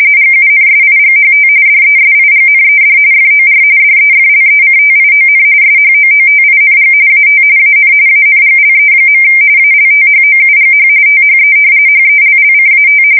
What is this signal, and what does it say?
RTTY, about 45.45 baud. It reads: GUNSIGHT DESIGNED FOR DEFLECTION SHOOTING AT AN ANGLE. ALTHOUGH THE HELLCAT
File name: fsk170 / usnavy4